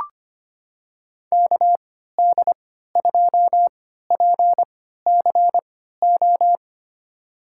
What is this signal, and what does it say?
Morse code, heard as a keyed tone, about 25 wpm, spaced as for 14 wpm, 700 Hz. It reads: KD2PCO
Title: USA Call Signs - 6 Characters - 3x Character Spacing - 25wpm